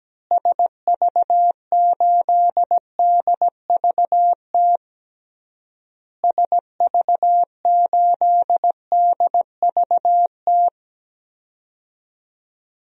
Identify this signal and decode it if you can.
Morse code, heard as a keyed tone, about 17 wpm, 700 Hz. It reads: SV8DVT SV8DVT
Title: DXCC-100 Call Signs - 6 Characters - Repeated 3x - 17wpm